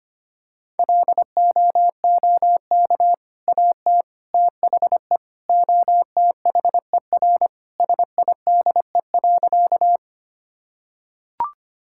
Morse code, heard as a keyed tone, about 25 wpm, 700 Hz. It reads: LOOK AT THE OTHER SIDE.
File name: practice / 25wpm / Sentences with 2-Word Phrases - 25wpm